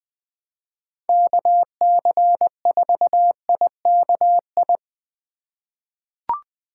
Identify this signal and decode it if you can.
Morse code, heard as a keyed tone, about 20 wpm, 700 Hz. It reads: KC4IKI
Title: USA Call Signs - 2x Character Spacing - 20wpm